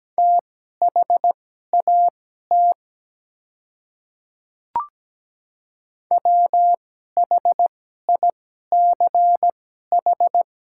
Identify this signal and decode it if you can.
Morse code, heard as a keyed tone, about 17 wpm, spaced as for 12 wpm, 700 Hz. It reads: THAT WHICH
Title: 2x Character Spacing - Top 100 Words - 17wpm